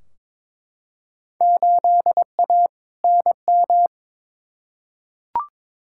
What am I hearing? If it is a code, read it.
Morse code, heard as a keyed tone, about 22 wpm, 700 Hz. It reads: 8A NM
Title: Field Day Exchanges Encore - 22wpm